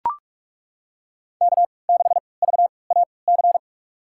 Morse code, heard as a keyed tone, about 43 wpm, spaced as for 25 wpm, 700 Hz. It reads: K6VAC
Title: USA Call Signs - 5 Characters - 3x Character Spacing - 45wpm